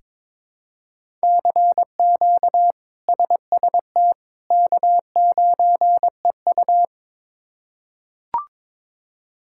Morse code, heard as a keyed tone, about 21 wpm, 700 Hz. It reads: CQ SST K9EU
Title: K1USN SST QSO Practice - 22wpm